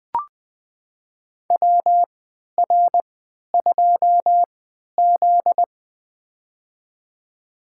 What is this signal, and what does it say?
Morse code, heard as a keyed tone, about 20 wpm, spaced as for 11 wpm, 700 Hz. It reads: WR2Z
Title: USA Call Signs - 3x Character Spacing - 20wpm